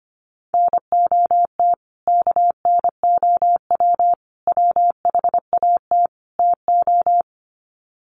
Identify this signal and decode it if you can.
Morse code, heard as a keyed tone, about 25 wpm, 700 Hz. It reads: NOT KNOW WHAT TO
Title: Sets of 4 Words 25wpm